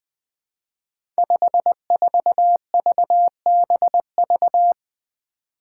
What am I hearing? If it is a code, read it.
Morse code, heard as a keyed tone, about 20 wpm, 700 Hz. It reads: 54VBV
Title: Five Challenging Letter-Number Combinations 20wpm